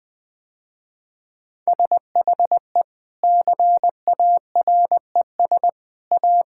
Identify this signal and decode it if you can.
Morse code, heard as a keyed tone, about 21 wpm, 700 Hz. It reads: SHE CARES A
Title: Sentences with 3-Word Phrases - Spoken Beginning - 20wpm